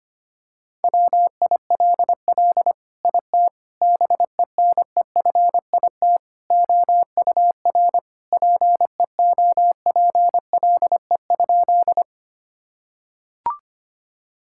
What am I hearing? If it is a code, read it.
Morse code, heard as a keyed tone, about 25 wpm, 700 Hz. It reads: WILL IT BENEFIT OUR PEOPLE?
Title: Sentences from Top 1000 Words - Review Part 6 - 25wpm